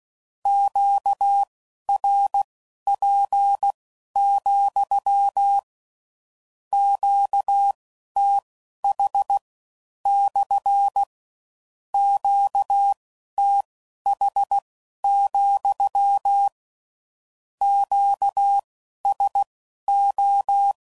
Morse code, heard as a keyed tone, about 16 wpm, spaced as for 11 wpm, 800 Hz. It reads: QRP, QTH / QTH, QSO
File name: CW Code Groups @15WPM / Q Calls / Q Calls SPSL